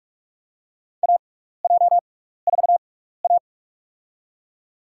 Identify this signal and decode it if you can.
Morse code, heard as a keyed tone, about 44 wpm, spaced as for 15 wpm, 700 Hz. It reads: AJ4A
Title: USA Call Signs - 4 Characters - 6x Character Spacing - 45wpm